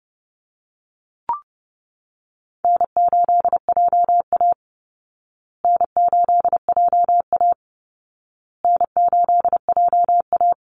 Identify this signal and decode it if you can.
Morse code, heard as a keyed tone, about 30 wpm, 700 Hz. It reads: N8JA N8JA N8JA
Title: Easy Call Signs Repeated 3x - 30wpm